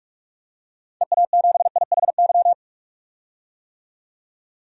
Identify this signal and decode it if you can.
Morse code, heard as a keyed tone, about 45 wpm, 700 Hz. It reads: EA7IHY
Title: DXCC-100 Call Signs - 6 Characters - 45wpm